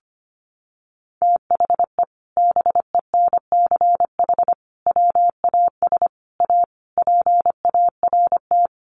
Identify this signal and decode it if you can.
Morse code, heard as a keyed tone, about 25 wpm, 700 Hz. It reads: THE BENCH WAS A PART
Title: Sentences from Top 1400 Words 25wpm